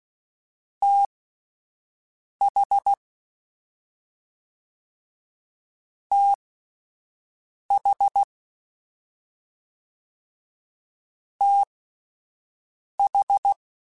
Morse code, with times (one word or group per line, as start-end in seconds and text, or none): 0.8-2.9 TH
6.1-8.2 TH
11.4-13.5 TH